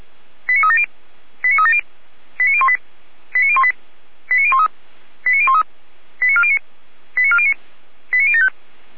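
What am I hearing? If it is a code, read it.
Five-tone selective call (ZVEI), heard as quick runs of five tones, tones of about 70 ms. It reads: 89370 89370 89018 89018 89013 89013 89409 89409 89075
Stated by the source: ZVEI five-tone call 89500 received by SDR, Feuerwehr Orlamünde.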